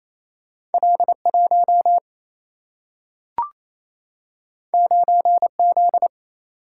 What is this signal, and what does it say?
Morse code, heard as a keyed tone, about 28 wpm, 700 Hz. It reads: L1 9Z
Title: Single-Letter Single-Number 28wpm